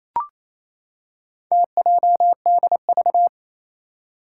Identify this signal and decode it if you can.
Morse code, heard as a keyed tone, about 27 wpm, 700 Hz. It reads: TJDV
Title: Four Challenging Letter-Number Combinations 28wpm